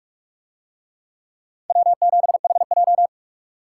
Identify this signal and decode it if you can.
Morse code, heard as a keyed tone, about 45 wpm, 700 Hz. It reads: W7HJ